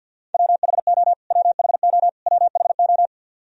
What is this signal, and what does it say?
Morse code, heard as a keyed tone, about 48 wpm, 700 Hz. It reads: WHO WHO WHO
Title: Top 100 Words Repeated 50wpm